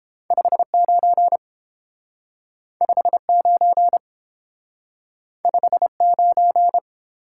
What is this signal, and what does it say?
Morse code, heard as a keyed tone, about 30 wpm, 700 Hz. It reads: 59 59 59